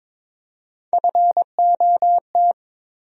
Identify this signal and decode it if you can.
Morse code, heard as a keyed tone, about 22 wpm, 700 Hz. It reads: FOT
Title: USA Call Sign Postfixes - 3 Letters 22wpm